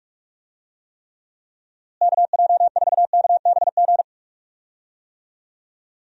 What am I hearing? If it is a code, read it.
Morse code, heard as a keyed tone, about 42 wpm, 700 Hz. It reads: KJ4KBG
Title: USA Call Signs - 6 Characters - Repeated 2x - 45wpm